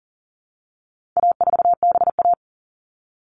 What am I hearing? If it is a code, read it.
Morse code, heard as a keyed tone, about 40 wpm, 700 Hz. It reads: A4BA
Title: All Cut Numbers - 4-Digit 40wpm